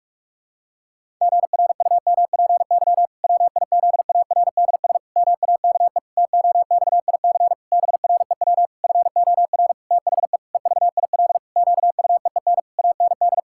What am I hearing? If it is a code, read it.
Morse code, heard as a keyed tone, about 44 wpm, 700 Hz. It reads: GRUMPY WIZARDS MAKE TOXIC BREW FOR THE EVIL QUEEN AND
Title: Pangram Challenge v2 - 45wpm